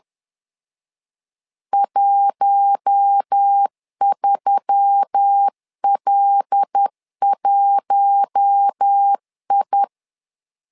Morse code, heard as a keyed tone, about 10 wpm, 800 Hz. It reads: 13L1I
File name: LEZIONI / 5 LBVC / 5 L 10wpm 2259